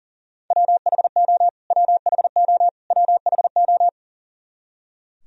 Morse code, heard as a keyed tone, about 37 wpm, 700 Hz. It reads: WHO WHO WHO
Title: Top 100 Words Repeated 40wpm